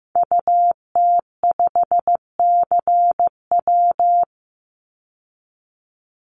Morse code, heard as a keyed tone, about 15 wpm, 700 Hz. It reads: UT5CW
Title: Call Signs Repeated 2x - 15wpm